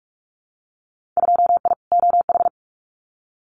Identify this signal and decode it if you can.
Morse code, heard as a keyed tone, about 44 wpm, 700 Hz. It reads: 2I OH
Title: Winter Field Day Exchanges 45wpm